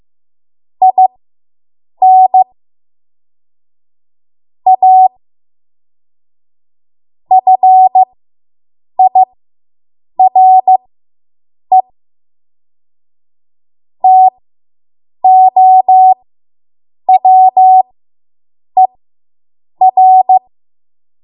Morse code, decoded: IN A FIRE TOWER